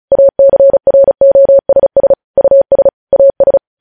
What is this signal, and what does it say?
Morse code, heard as a keyed tone, about 36 wpm, 550 Hz. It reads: ACROSS US AS
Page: news headlines in morse code at 35 wpm